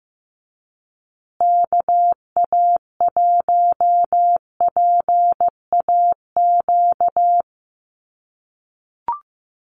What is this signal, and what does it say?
Morse code, heard as a keyed tone, about 15 wpm, 700 Hz. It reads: KA1PAQ
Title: USA Call Signs Repeated 3x 15wpm